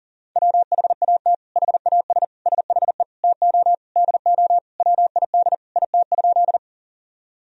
Morse code, heard as a keyed tone, about 40 wpm, 700 Hz. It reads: WHAT HAS SHE TO DO WID IT?
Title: Sentences from Top 100 Words and Abbreviations 40wpm